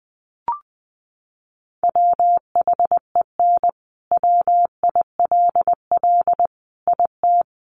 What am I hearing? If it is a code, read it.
Morse code, heard as a keyed tone, about 20 wpm, 700 Hz. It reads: WHEN WILL IT
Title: Sentences from Top 200 Words 20wpm